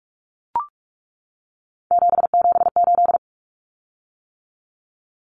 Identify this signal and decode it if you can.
Morse code, heard as a keyed tone, about 44 wpm, 700 Hz. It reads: 778
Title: Common Cut Numbers - 3-Digit 45wpm